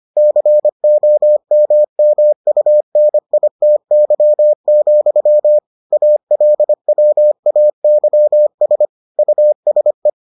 Morse code, heard as a keyed tone, about 25 wpm, 600 Hz. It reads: COMMUNITY, ALWAYS USE